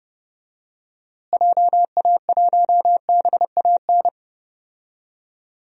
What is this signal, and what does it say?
Morse code, heard as a keyed tone, about 30 wpm, 700 Hz. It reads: JA1BAN
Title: DXCC-100 Call Signs - 6 Characters - Repeated 2x - 30wpm